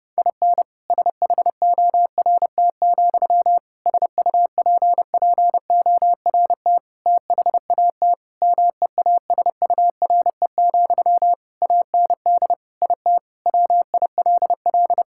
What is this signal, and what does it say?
Morse code, heard as a keyed tone, about 30 wpm, 700 Hz. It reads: IN SHORT, SUPPORT THAT MEASURE, AND IT WILL
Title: Sentences from Top 1000 Words - Review Part 5 - 30wpm